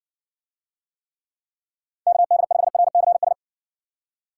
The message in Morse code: KD5RCS

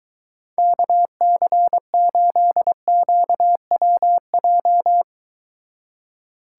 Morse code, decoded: KC8QWJ

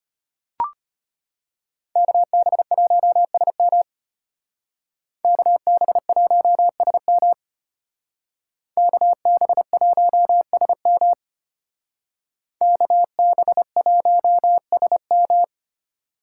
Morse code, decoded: KB1SM KB1SM KB1SM KB1SM

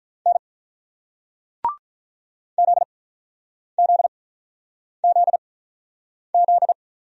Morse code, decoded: N ZZZZ

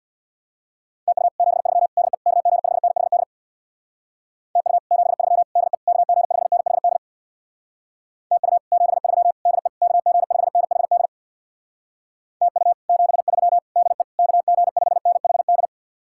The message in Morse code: TU 73 DE KG5NHD TU 73 DE KG5NHD TU 73 DE KG5NHD TU 73 DE KG5NHD

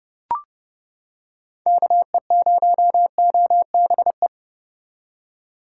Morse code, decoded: KE0OBE

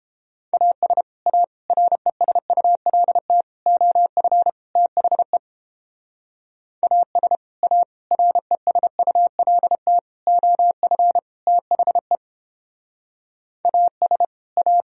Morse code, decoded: AS A RESULT OF THE AS A RESULT OF THE AS A

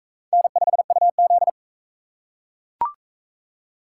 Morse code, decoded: N5UZ

